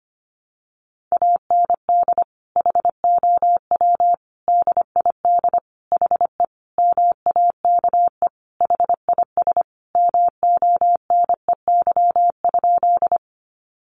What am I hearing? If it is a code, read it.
Morse code, heard as a keyed tone, about 25 wpm, 700 Hz. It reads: AND HOW DID HE MAKE HIS MONEY?
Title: Sentences from Top 400 Words 25wpm